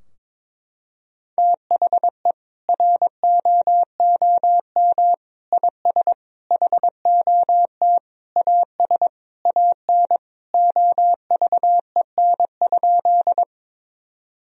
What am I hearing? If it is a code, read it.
Morse code, heard as a keyed tone, about 22 wpm, 700 Hz. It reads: THE ROOM IS HOT AS AN OVEN?
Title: Sentences from Top 1000 Words - Review Part 4 - 22wpm